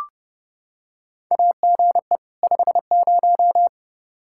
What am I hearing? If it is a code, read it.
Morse code, heard as a keyed tone, about 31 wpm, 700 Hz. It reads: AGE 50
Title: Practice QSO - Exchange 3 - More Info - 30wpm